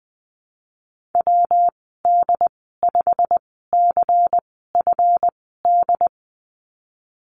Morse code, heard as a keyed tone, about 20 wpm, spaced as for 15 wpm, 700 Hz. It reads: WD5CFD
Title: 173 Lesson 28g - Number 9 - Call Signs 20wpm